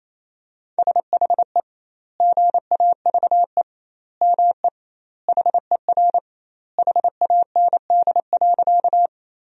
Morse code, heard as a keyed tone, about 28 wpm, 700 Hz. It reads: SHE GAVE ME HER HAND.
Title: 2x Word-Spacing - Sentences from Top 500 Words - 28wpm